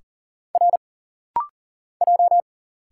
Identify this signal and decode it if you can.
Morse code, heard as a keyed tone, about 40 wpm, 700 Hz. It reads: R J
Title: Single Letters - Rapid-Fire 40wpm